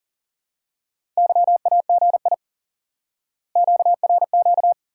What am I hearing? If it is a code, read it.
Morse code, heard as a keyed tone, about 40 wpm, 700 Hz. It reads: YAGI QRQ
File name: course / instant-qso-element-course-3 / IQE Course 3 - Lesson 89 - YAGI - Two Elements - 40wpm